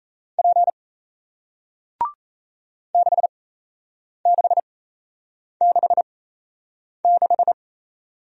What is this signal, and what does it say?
Morse code, heard as a keyed tone, about 38 wpm, spaced as for 8 wpm, 700 Hz. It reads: P 6666